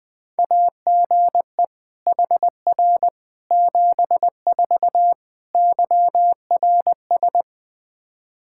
Morse code, heard as a keyed tone, about 20 wpm, 700 Hz. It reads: AGE HR 74 YRS
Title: Practice QSO - Exchange 3 - More Info - 20wpm